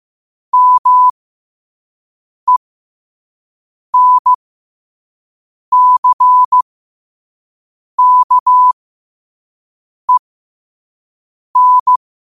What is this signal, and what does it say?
Morse code, heard as a keyed tone, about 15 wpm, spaced as for 5 wpm, 1000 Hz. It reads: MENCKEN